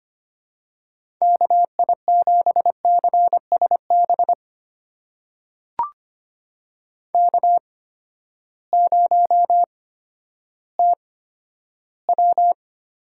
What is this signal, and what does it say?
Morse code, heard as a keyed tone, about 25 wpm, spaced as for 12 wpm, 700 Hz. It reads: KI7CSB K0TW
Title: USA Call Signs - 8x Character Spacing - 25wpm